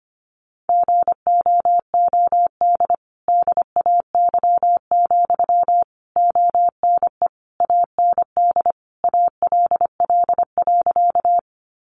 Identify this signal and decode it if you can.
Morse code, heard as a keyed tone, about 25 wpm, 700 Hz. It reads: GOOD DAY, ONE AND ALL.
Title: Sentences from Top 200 Words 25wpm